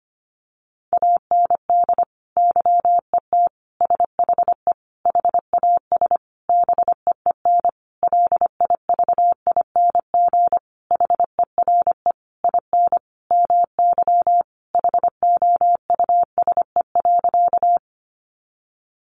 Morse code, heard as a keyed tone, about 25 wpm, 700 Hz. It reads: AND YET SHE HAS BEEN LIVING HERE IN MY HOUSE.
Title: Sentences from Top 1100 Words 25wpm